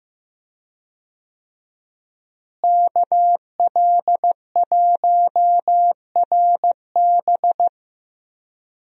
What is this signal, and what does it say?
Morse code, heard as a keyed tone, about 15 wpm, 700 Hz. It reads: KL1RB
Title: USA Call Signs - 5 Characters - Repeated 3x - 15wpm